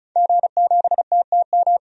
Morse code, heard as a keyed tone, about 36 wpm, 700 Hz. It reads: G7TTM
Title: DXCC-100 Call Signs - 5 Characters - 35wpm